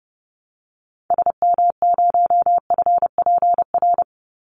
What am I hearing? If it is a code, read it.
Morse code, heard as a keyed tone, about 30 wpm, 700 Hz. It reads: SM0FPR
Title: Call Signs Repeated 3x - 30wpm